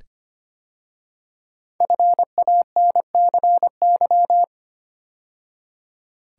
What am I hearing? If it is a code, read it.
Morse code, heard as a keyed tone, about 25 wpm, 700 Hz. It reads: FANCY
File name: 5 Letter Words - First 3 Letters Given - 25wpm